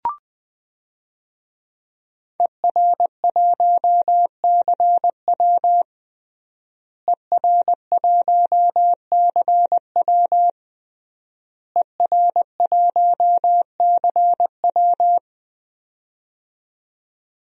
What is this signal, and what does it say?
Morse code, heard as a keyed tone, about 20 wpm, 700 Hz. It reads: ER1CW ER1CW ER1CW
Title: DXCC-100 Call Signs - 5 Characters - Repeated 3x - 20wpm